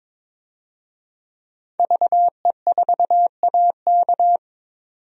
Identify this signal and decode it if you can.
Morse code, heard as a keyed tone, about 21 wpm, 700 Hz. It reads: VE4AK
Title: DXCC-100 Call Signs - 5 Characters - 22wpm